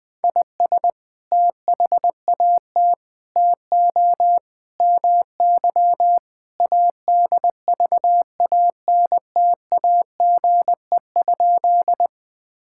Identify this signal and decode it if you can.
Morse code, heard as a keyed tone, about 20 wpm, 700 Hz. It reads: IS THAT TO MY ADVANTAGE?
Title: Sentences from Top 1000 Words - Review Part 6 - 20wpm